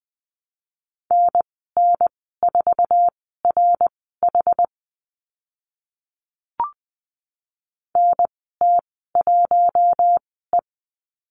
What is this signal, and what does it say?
Morse code, heard as a keyed tone, about 20 wpm, spaced as for 14 wpm, 700 Hz. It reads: NN4RH NT1E
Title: 076 Lesson 13g - Letter L - Call Signs 20wpm